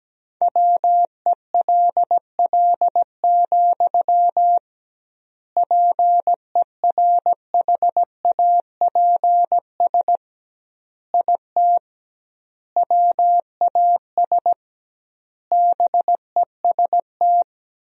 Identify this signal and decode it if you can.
Morse code, heard as a keyed tone, about 17 wpm, 700 Hz. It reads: WELL, PERHAPS IT WAS BEST
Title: 2x Word-Spacing - Sentences from Top 500 Words - 17wpm